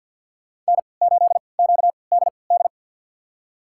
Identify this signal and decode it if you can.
Morse code, heard as a keyed tone, about 53 wpm, spaced as for 29 wpm, 700 Hz. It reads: N8QDD
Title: USA Call Signs - 3x Character Spacing - 50wpm